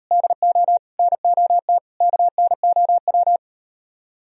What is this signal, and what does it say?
Morse code, heard as a keyed tone, about 38 wpm, 700 Hz. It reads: DO NOT KNOW